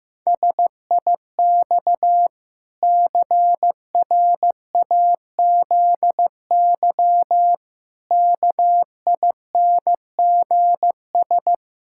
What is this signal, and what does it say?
Morse code, heard as a keyed tone, about 15 wpm, 700 Hz. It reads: SIX CRAZY KINGS
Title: Pangram Challenge v2 - 15wpm